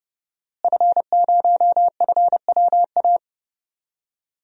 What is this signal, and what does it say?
Morse code, heard as a keyed tone, about 30 wpm, 700 Hz. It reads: F0FWA
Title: DXCC-100 Call Signs - 5 Characters - Repeated 2x - 30wpm